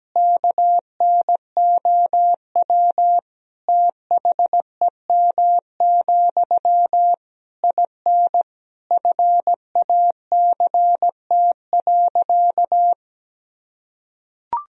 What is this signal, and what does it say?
Morse code, heard as a keyed tone, about 17 wpm, 700 Hz. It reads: KNOW THEM, IN FACT.
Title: Sentences from Top 1000 Words - Review Part 3 - 17wpm